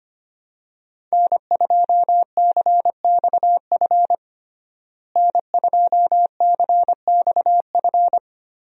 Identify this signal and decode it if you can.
Morse code, heard as a keyed tone, about 25 wpm, 700 Hz. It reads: N2CXF N2CXF
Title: USA Call Signs - 5 Characters - Repeated 3x - 25wpm